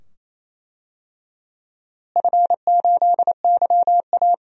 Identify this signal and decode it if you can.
Morse code, heard as a keyed tone, about 28 wpm, 700 Hz. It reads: F8YA
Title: DXCC-100 Call Signs - 4 Characters - Repeated 2x - 28wpm